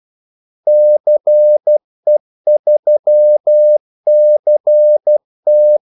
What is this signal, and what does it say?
Morse code, heard as a keyed tone, about 12 wpm, 600 Hz. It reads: CE3CT
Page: 2 ekstra ordmellemrum mellem kaldesignalerne